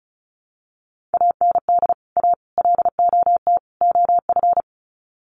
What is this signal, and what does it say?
Morse code, heard as a keyed tone, about 34 wpm, 700 Hz. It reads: AND A LOT OF
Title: Sets of 4 Words 35wpm